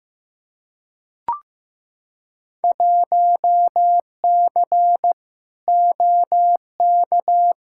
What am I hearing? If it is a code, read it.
Morse code, heard as a keyed tone, about 15 wpm, 700 Hz. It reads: 1C OK
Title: Field Day Exchanges Encore - 15wpm